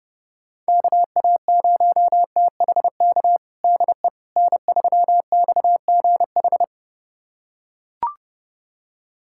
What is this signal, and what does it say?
Morse code, heard as a keyed tone, about 30 wpm, 690 Hz. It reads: KA0THK DE N3XGH
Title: USA Callsign DE Callsign 30wpm